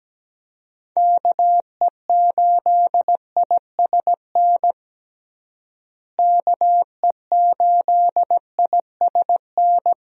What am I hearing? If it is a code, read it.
Morse code, heard as a keyed tone, about 17 wpm, 700 Hz. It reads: KE8ISN KE8ISN
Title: USA Call Signs - 6 Characters - Repeated 2x - 17wpm